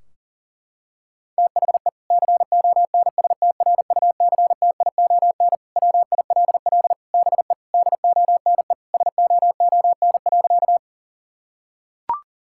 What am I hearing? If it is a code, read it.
Morse code, heard as a keyed tone, about 40 wpm, 700 Hz. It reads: THE CONSTRUCTION WILL BE DONE SOON.
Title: Sentences from Top 1600 Words 40wpm